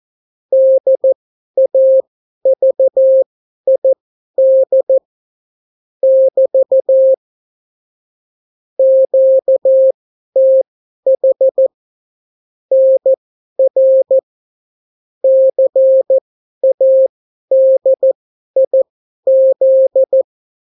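Morse code, heard as a keyed tone, about 14 wpm, spaced as for 11 wpm, 530 Hz. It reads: DAVID <BT> QTH NR CADIZ
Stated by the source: Example Spanish CW QSO from a Morse learning book, 14 wpm characters with 8 wpm spacing.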